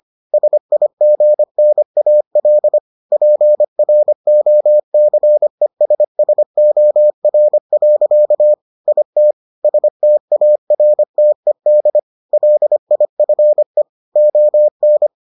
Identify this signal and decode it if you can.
Morse code, heard as a keyed tone, about 25 wpm, 600 Hz. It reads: SIGNAL PROCESSOR. IT STARTED LIFE ON